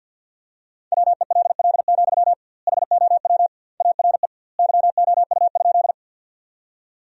Morse code, heard as a keyed tone, about 50 wpm, 700 Hz. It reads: WELL, HOW ARE YOU?
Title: 2x Word-Spacing - Sentences from Top 200 Words - 50wpm